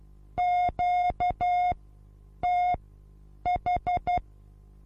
Morse code, heard as a keyed tone, about 12 wpm, spaced as for 8 wpm, 720 Hz. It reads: QTH
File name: Onlinelehrgang / morsen / DJ4UF Morsen23